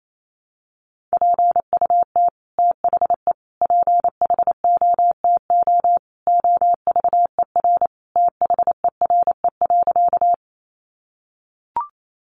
Morse code, heard as a keyed tone, about 28 wpm, 700 Hz. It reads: PUT THE PHOTO OVER THERE.
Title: Sentences from Top 1200 Words 28wpm